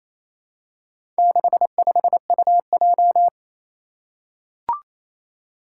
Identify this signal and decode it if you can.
Morse code, heard as a keyed tone, about 27 wpm, 700 Hz. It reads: 65UJ
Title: Four Challenging Letter-Number Combinations 28wpm